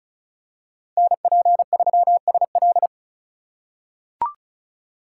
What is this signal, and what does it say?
Morse code, heard as a keyed tone, about 36 wpm, 700 Hz. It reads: NP3SL
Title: USA Call Signs - 5 Characters - 4x Character Spacing - 35wpm